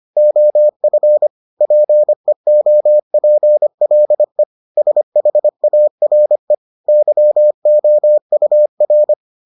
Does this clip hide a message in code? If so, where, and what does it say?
Yes: Morse code from 0.2 to 9.1 s, heard as a keyed tone, about 25 wpm, 600 Hz. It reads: OF PEOPLE SHARE YOUR